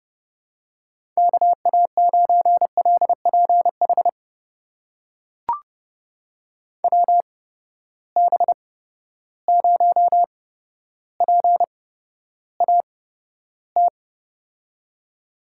Morse code, heard as a keyed tone, about 29 wpm, spaced as for 13 wpm, 700 Hz. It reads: KA9LPH WB0PAT
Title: USA Call Signs - 6 Characters - 8x Character Spacing - 30wpm